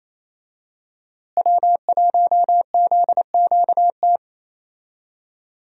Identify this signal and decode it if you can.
Morse code, heard as a keyed tone, about 27 wpm, 700 Hz. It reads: W1ZQT
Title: USA Call Signs - 5 Characters - Repeated 2x - 28wpm